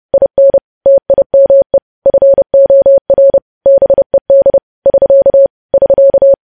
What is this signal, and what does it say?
Morse code, heard as a keyed tone, about 30 wpm, 550 Hz. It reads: IN TIME FOR BED <SK> <SK>
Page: news headlines in morse code at 30 wpm